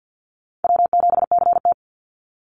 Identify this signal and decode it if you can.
Morse code, heard as a keyed tone, about 50 wpm, 700 Hz. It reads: W7CT
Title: Easy Call Signs Repeated 3x - 50wpm